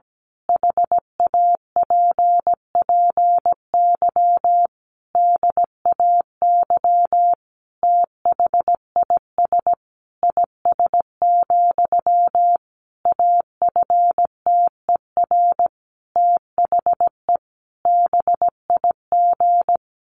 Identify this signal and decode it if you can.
Morse code, heard as a keyed tone, about 17 wpm, 700 Hz. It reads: HAPPY DAY THIS IS, AFTER THE BIG
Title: Sentences from Top 1500 Words 17wpm